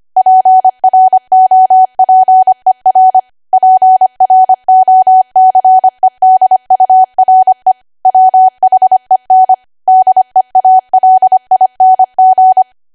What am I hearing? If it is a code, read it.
Morse code, heard as a keyed tone, about 25 wpm, 750 Hz. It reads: PROPER PROCEDURE WHEN DEALING
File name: Archive / 25 WPM / 25WPM